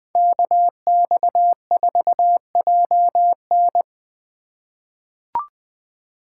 Morse code, heard as a keyed tone, about 21 wpm, 700 Hz. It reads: KX4JN
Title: USA Call Signs - 5 Characters - 2x Character Spacing - 20wpm